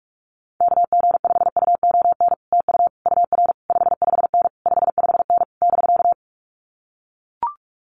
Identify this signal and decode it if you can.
Morse code, heard as a keyed tone, about 44 wpm, 700 Hz. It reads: KG5UON TU UR 55N 55N <BK>
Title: Common QSO Phrases Encore 45wpm